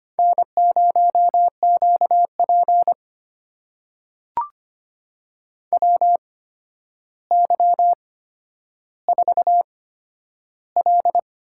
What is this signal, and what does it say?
Morse code, heard as a keyed tone, about 25 wpm, spaced as for 11 wpm, 700 Hz. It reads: N0QP WY4L
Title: USA Call Signs - 4 Characters - 8x Character Spacing - 25wpm